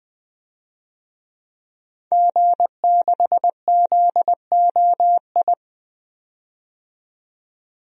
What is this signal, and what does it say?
Morse code, heard as a keyed tone, about 20 wpm, 700 Hz. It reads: G6ZOI
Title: DXCC-100 Call Signs - 5 Characters - 20wpm